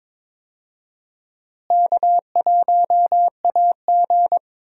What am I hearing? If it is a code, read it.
Morse code, heard as a keyed tone, about 22 wpm, 700 Hz. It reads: K1AG
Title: USA Call Signs - 4 Characters - Repeated 3x - 22wpm